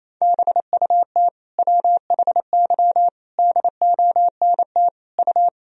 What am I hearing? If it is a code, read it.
Morse code, heard as a keyed tone, about 28 wpm, 700 Hz. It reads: BUT WHY DONT U
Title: Sentences from Top 300 Words and Abbreviations 28wpm